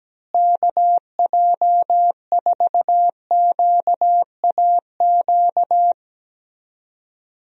Morse code, heard as a keyed tone, about 17 wpm, 700 Hz. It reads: KJ4QAQ
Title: USA Call Signs - 6 Characters - Repeated 2x - 17wpm